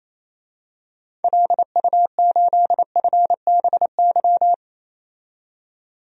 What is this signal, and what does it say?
Morse code, heard as a keyed tone, about 27 wpm, 700 Hz. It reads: LU8FBY
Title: DXCC-100 Call Signs - 6 Characters - 28wpm